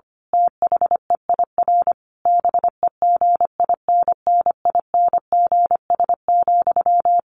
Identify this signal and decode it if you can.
Morse code, heard as a keyed tone, about 25 wpm, 700 Hz. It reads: THEIR BEGINNINGS,
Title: A Sound Of Thunder 25wpm